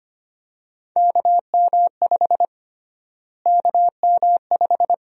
Morse code, heard as a keyed tone, about 25 wpm, 700 Hz. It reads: KM5 KM5
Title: USA Call Sign Prefixes - Repeated 2x - 25wpm